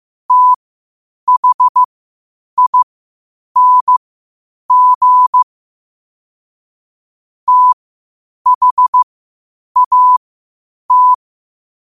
Morse code, heard as a keyed tone, about 15 wpm, spaced as for 8 wpm, 1000 Hz. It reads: THING THAT
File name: qotdfiles / 8wpm